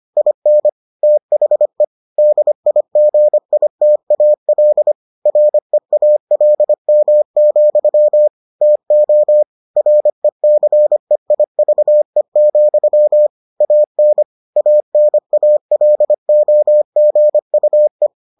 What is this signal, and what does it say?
Morse code, heard as a keyed tone, about 25 wpm, 600 Hz. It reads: IN THE DIGITAL REALM, TO RECEIVE, AN ANALOGUE